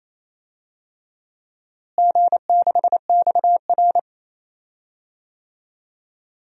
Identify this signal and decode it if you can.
Morse code, heard as a keyed tone, about 28 wpm, 700 Hz. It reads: G6XR
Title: DXCC-100 Call Signs - 4 Characters - Repeated 3x - 28wpm